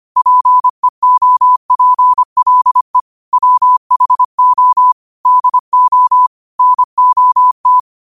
Morse code, heard as a keyed tone, about 25 wpm, 1000 Hz. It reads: PEOPLE WHO DO NOT
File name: qotdfiles / 25wpm